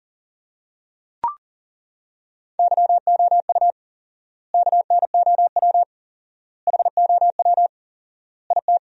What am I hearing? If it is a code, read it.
Morse code, heard as a keyed tone, about 37 wpm, 700 Hz. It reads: YOU KNOW HOW IT
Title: 4x Word-Spacing - Sentences from Top 100 Words - 40wpm